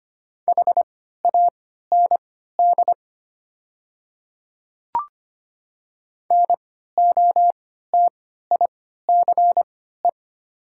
Morse code, heard as a keyed tone, about 25 wpm, spaced as for 14 wpm, 700 Hz. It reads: HAND NOTICE
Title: 3x Character Spacing - Top 500 Words - 25wpm